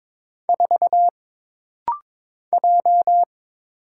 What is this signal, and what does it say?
Morse code, heard as a keyed tone, about 22 wpm, 700 Hz. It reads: 4 J
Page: This rapid-fire timing is geared to push your ICR (Instant Character Recognition) skills.